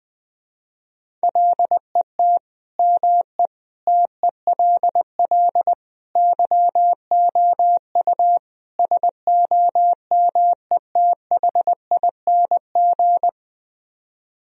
Morse code, heard as a keyed tone, about 20 wpm, 700 Hz. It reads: LET ME TELL YOU SOMETHING
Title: Speed-Racing - Sets of 5 Words 20wpm